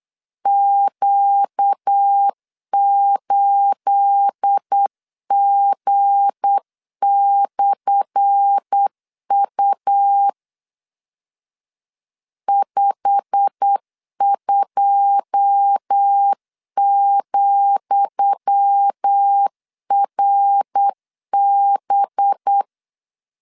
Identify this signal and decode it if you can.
Morse code, heard as a keyed tone, about 8 wpm, 800 Hz. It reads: Q8G/U 52,RB
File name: LEZIONI / 10 SIGNS / 10D COMA 8wpm 3749